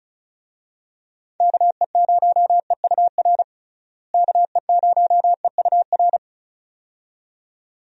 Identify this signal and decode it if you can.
Morse code, heard as a keyed tone, about 36 wpm, 700 Hz. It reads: KE0EUR KE0EUR
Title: USA Call Signs - 6 Characters - Repeated 2x - 35wpm